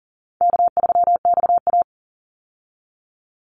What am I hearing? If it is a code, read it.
Morse code, heard as a keyed tone, about 40 wpm, 700 Hz. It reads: K3XA
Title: Call Signs Repeated 3x - 40wpm